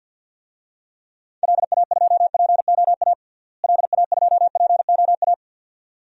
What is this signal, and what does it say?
Morse code, heard as a keyed tone, about 47 wpm, 700 Hz. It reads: LA2POA LA2POA
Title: DXCC-100 Call Signs - 6 Characters - Repeated 2x - 50wpm